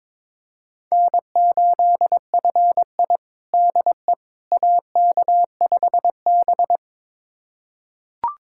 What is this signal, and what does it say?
Morse code, heard as a keyed tone, about 22 wpm, 700 Hz. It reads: N8FI DE AK5B
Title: USA Callsign DE Callsign - 4 Characters - 22wpm